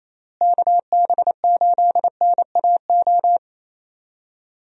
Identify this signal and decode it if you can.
Morse code, heard as a keyed tone, about 28 wpm, 700 Hz. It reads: KB8NAO
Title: USA Call Signs - 6 Characters - Repeated 2x - 28wpm